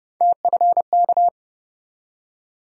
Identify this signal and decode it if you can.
Morse code, heard as a keyed tone, about 30 wpm, 700 Hz. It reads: TFK
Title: USA Call Sign Postfixes - 3 Letters 30wpm